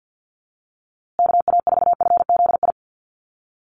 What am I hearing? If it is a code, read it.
Morse code, heard as a keyed tone, about 49 wpm, 700 Hz. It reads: KA4FZI
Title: USA Call Signs Repeated 2x 50wpm